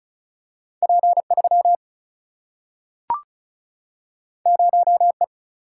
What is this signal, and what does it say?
Morse code, heard as a keyed tone, about 34 wpm, 700 Hz. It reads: P3 0E
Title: Single-Letter Single-Number 35wpm